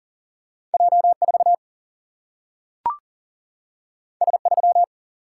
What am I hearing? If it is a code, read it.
Morse code, heard as a keyed tone, about 40 wpm, 700 Hz. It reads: J4 S3
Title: Single-Letter Single-Number 40wpm